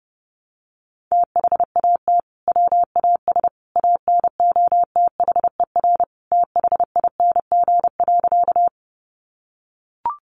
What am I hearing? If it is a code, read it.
Morse code, heard as a keyed tone, about 30 wpm, 700 Hz. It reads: THAT WAS ANOTHER THING.
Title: Sentences from Top 600 Words 30wpm